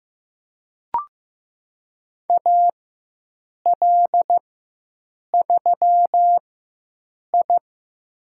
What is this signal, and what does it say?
Morse code, heard as a keyed tone, about 15 wpm, spaced as for 7 wpm, 700 Hz. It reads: AL3I
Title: USA Call Signs - 4 Characters - 4x Character Spacing - 15wpm